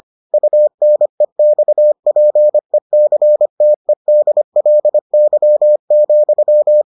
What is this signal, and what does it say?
Morse code, heard as a keyed tone, about 25 wpm, 600 Hz. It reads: UNEXPECTEDLY,